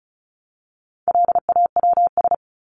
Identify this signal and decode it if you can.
Morse code, heard as a keyed tone, about 34 wpm, 700 Hz. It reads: LAWS
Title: ES:S Suffix - Top 100 Words with Base Words 35wpm